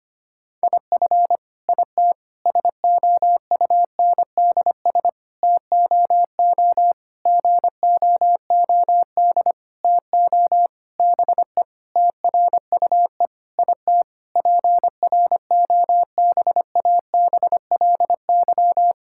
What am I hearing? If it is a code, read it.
Morse code, heard as a keyed tone, about 25 wpm, 700 Hz. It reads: IF IT SOUNDS TOO GOOD TO BE TRUE IT PROBABLY
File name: practice / 25wpm / Sentences with 9-Word Phrases - 25wpm